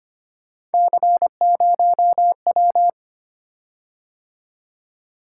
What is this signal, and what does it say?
Morse code, heard as a keyed tone, about 25 wpm, 700 Hz. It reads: C0W
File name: Three Letter-Number Combinations - 25wpm